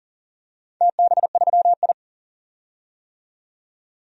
Morse code, heard as a keyed tone, about 38 wpm, 700 Hz. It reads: TB3I
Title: DXCC-100 Call Signs - 4 Characters - 40wpm